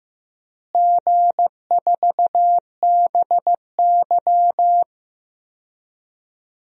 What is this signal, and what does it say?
Morse code, heard as a keyed tone, about 15 wpm, 700 Hz. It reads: G4BY
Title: DXCC-100 Call Signs - 4 Characters - Repeated 3x - 15wpm